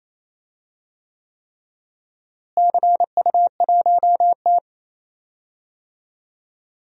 Morse code, heard as a keyed tone, about 28 wpm, 700 Hz. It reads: CU1T
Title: DXCC-100 Call Signs - 4 Characters - Repeated 3x - 28wpm